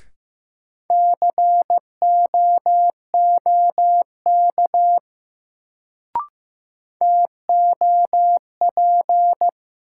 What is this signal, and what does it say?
Morse code, heard as a keyed tone, about 16 wpm, 700 Hz. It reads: COOK TOP
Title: Sentences from Top 700 Words - New Word Drill - 15wpm